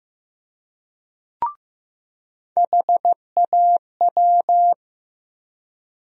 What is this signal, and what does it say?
Morse code, heard as a keyed tone, about 15 wpm, 700 Hz. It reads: HAW
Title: USA Call Sign Postfixes - 3 Letters 15wpm